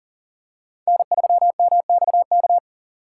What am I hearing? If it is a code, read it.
Morse code, heard as a keyed tone, about 40 wpm, 700 Hz. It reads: N3MXK
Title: USA Call Signs - 5 Characters - 6x Character Spacing - 40wpm